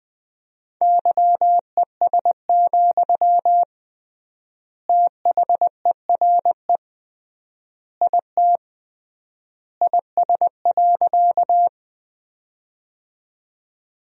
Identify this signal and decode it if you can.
Morse code, heard as a keyed tone, about 20 wpm, 700 Hz. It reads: YES, THERE IT IS.
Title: 3x Word-Spacing - Sentences from Top 500 Words - 20wpm